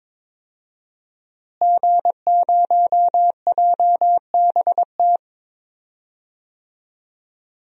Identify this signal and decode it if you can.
Morse code, heard as a keyed tone, about 22 wpm, 700 Hz. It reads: G0JBT
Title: DXCC-100 Call Signs - 5 Characters - Repeated 3x - 22wpm